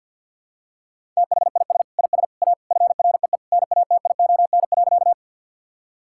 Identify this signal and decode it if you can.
Morse code, heard as a keyed tone, about 47 wpm, 700 Hz. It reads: THIS IS A FREE NATION.
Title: Sentences from Top 1000 Words - Review Part 6 - 50wpm